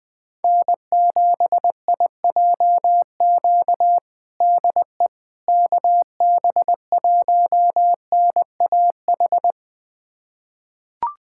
Morse code, heard as a keyed tone, about 21 wpm, 700 Hz. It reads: N7IJQ DE KB1NAH
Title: USA Callsign DE Callsign 20wpm